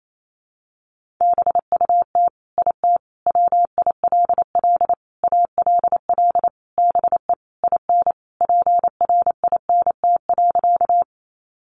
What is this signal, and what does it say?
Morse code, heard as a keyed tone, about 28 wpm, 700 Hz. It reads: BUT IT WILL ALL BE IN PRINT.
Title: Sentences from Top 1400 Words 28wpm